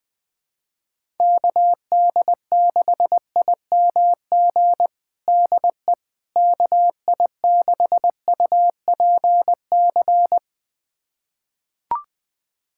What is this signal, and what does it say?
Morse code, heard as a keyed tone, about 20 wpm, 700 Hz. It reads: KD6IMG DE KI6UPC
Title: USA Callsign DE Callsign 20wpm